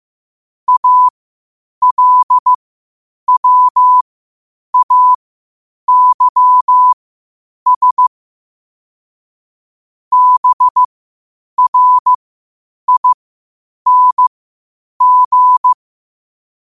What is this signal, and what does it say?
Morse code, heard as a keyed tone, about 15 wpm, spaced as for 8 wpm, 1000 Hz. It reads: ALWAYS BRING
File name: qotdfiles / 8wpm